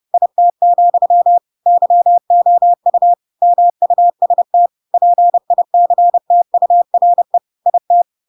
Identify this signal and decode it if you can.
Morse code, heard as a keyed tone, about 30 wpm, 700 Hz. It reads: IT, YOU MUST PICTURE IT
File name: Pinocchio / Pinocchio ch5 30 30 ik1hge